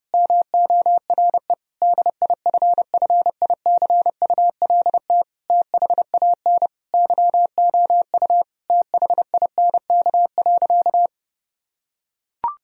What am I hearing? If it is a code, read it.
Morse code, heard as a keyed tone, about 30 wpm, 700 Hz. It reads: MORE DIFFICULT THAN YOU THINK.
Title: Sentences from Top 1000 Words - Review Part 3 - 30wpm